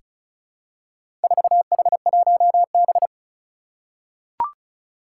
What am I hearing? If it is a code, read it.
Morse code, heard as a keyed tone, about 35 wpm, 700 Hz. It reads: 4H1B
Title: Four Challenging Letter-Number Combinations 35wpm